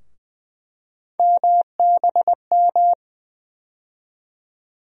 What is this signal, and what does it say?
Morse code, heard as a keyed tone, about 21 wpm, 700 Hz. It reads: MBM
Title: USA Call Sign Postfixes - 3 Letters 20wpm